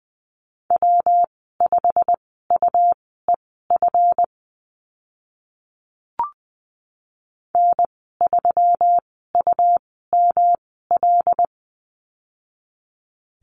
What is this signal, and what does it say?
Morse code, heard as a keyed tone, about 20 wpm, spaced as for 15 wpm, 700 Hz. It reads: W5UEF N3UML
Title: 147 Lesson 24g - Letter y - Call Signs 20wpm